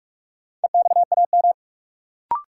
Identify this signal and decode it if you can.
Morse code, heard as a keyed tone, about 46 wpm, 700 Hz. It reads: EXAM
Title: Sentences from Top 1000 Words - Review Part 5 - New Word Drill - 45wpm